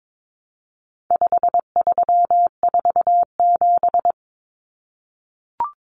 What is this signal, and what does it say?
Morse code, heard as a keyed tone, about 22 wpm, 700 Hz. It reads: 5347
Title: Common Cut Numbers - 4-Digit 22wpm